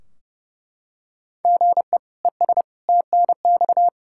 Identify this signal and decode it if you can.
Morse code, heard as a keyed tone, about 31 wpm, 700 Hz. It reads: GE ES TNX
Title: Practice QSO - Exchange 1 - Beginning - 30wpm